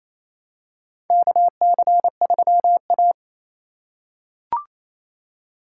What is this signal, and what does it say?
Morse code, heard as a keyed tone, about 28 wpm, 700 Hz. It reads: KC3A
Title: USA Call Signs - 4 Characters - 4x Character Spacing - 28wpm